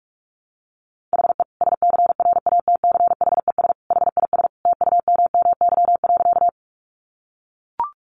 Morse code, heard as a keyed tone, about 44 wpm, 700 Hz. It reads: HE SCRATCHES HIS TUMMY.
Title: Easy Phrases 45wpm